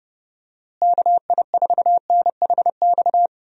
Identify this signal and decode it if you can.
Morse code, heard as a keyed tone, about 30 wpm, 700 Hz. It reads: KI4NHX
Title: USA Call Signs - 6 Characters - Repeated 2x - 30wpm